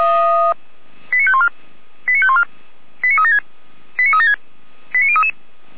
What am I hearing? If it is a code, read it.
Five-tone selective call (ZVEI), heard as quick runs of five tones, tones of about 70 ms. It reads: 89514 89514 89376 89376 89033
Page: SDR/ZVEI